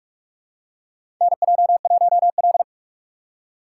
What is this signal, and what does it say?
Morse code, heard as a keyed tone, about 46 wpm, 700 Hz. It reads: NJ1L
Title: USA Call Signs - 4 Characters - 6x Character Spacing - 45wpm